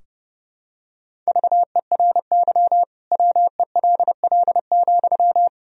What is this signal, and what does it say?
Morse code, heard as a keyed tone, about 32 wpm, 700 Hz. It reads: VERY WELL,
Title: Sentences from Top 1000 Words - Review Part 4 - 30wpm